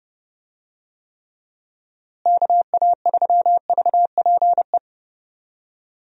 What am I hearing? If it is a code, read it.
Morse code, heard as a keyed tone, about 32 wpm, 700 Hz. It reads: KA3VPE
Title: USA Call Signs - 6 Characters - Repeated 3x - 30wpm